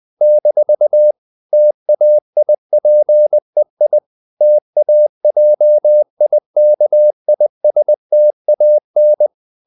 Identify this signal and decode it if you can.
Morse code, heard as a keyed tone, about 20 wpm, 600 Hz. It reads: - TAIPEI TAJIKISTAN
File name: CW / Practice / NationalCapitals20